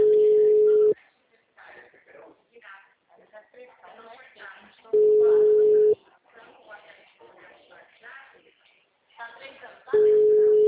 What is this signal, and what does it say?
Ringback tone (Europe, 425 Hz), with speech.
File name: out- -2014- - - .52600